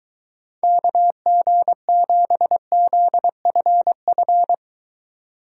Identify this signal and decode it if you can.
Morse code, heard as a keyed tone, about 23 wpm, 700 Hz. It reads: KG7ZFF